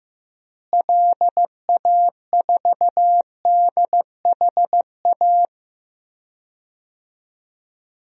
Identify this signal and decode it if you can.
Morse code, heard as a keyed tone, about 15 wpm, 700 Hz. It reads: LA4DHA
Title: DXCC-100 Call Signs - 6 Characters - Repeated 3x - 15wpm